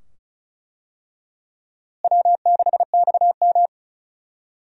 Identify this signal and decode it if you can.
Morse code, heard as a keyed tone, about 34 wpm, 700 Hz. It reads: W6XM
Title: USA Call Signs - 4 Characters - Repeated 2x - 35wpm